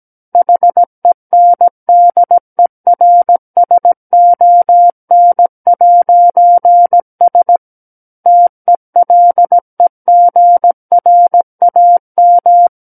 Morse code, heard as a keyed tone, about 17 wpm, 700 Hz. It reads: HENDERSON'S TELEGRAM
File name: Book One / 08-Chapter 8 - 17 WPM